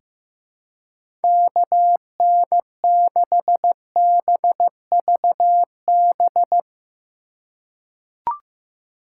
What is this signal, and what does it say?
Morse code, heard as a keyed tone, about 15 wpm, 700 Hz. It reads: KN6BVB
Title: USA Call Signs - 2x Character Spacing - 15wpm